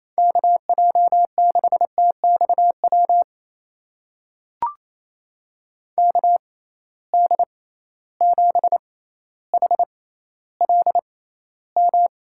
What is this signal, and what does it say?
Morse code, heard as a keyed tone, about 28 wpm, spaced as for 14 wpm, 700 Hz. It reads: KJ6TXW KD7HLM
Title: USA Call Signs - 6 Characters - 6x Character Spacing - 28wpm